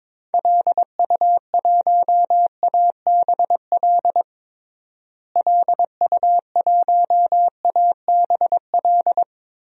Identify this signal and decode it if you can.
Morse code, heard as a keyed tone, about 22 wpm, 700 Hz. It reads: LU1ABL LU1ABL
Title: DXCC-100 Call Signs - 6 Characters - Repeated 3x - 22wpm